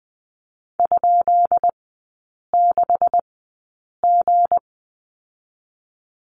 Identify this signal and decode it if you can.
Morse code, heard as a keyed tone, about 20 wpm, spaced as for 8 wpm, 690 Hz. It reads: ?6G
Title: 159 Lesson 26e - Letter g - Three Characters 20wpm